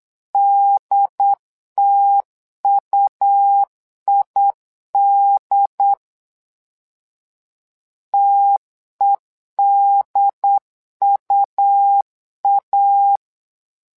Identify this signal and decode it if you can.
Morse code, heard as a keyed tone, about 8 wpm, 800 Hz. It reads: DTUID TEDUA
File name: LEZIONI / 2 UHDN / 2 UHD 8wpm 2258